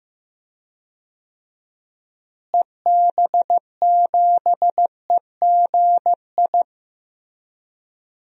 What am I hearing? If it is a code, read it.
Morse code, heard as a keyed tone, about 16 wpm, 700 Hz. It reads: EB7EGI
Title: DXCC-100 Call Signs - 6 Characters - Repeated 3x - 15wpm